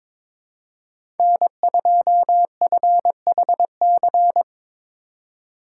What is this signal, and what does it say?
Morse code, heard as a keyed tone, about 22 wpm, 700 Hz. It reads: N2FHC
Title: USA Call Signs - 5 Characters - 3x Character Spacing - 22wpm